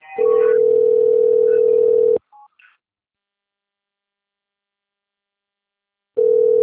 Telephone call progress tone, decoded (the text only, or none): ringback tone